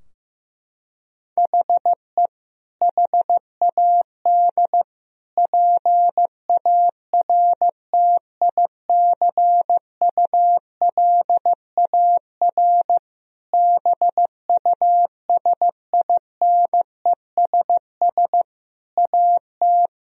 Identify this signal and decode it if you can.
Morse code, heard as a keyed tone, about 15 wpm, 700 Hz. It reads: HE HAD PARTICULAR BUSINESS AT
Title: Sentences from Top 1000 Words - Review Part 4 - 15wpm